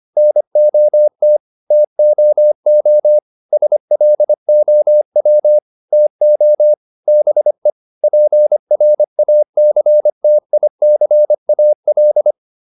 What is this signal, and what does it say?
Morse code, heard as a keyed tone, about 25 wpm, 600 Hz. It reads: NOT TOO SLOW TO BE PRACTICAL